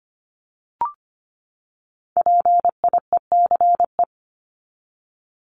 Morse code, heard as a keyed tone, about 25 wpm, 700 Hz. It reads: PIECE
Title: Top 500 Words 25wpm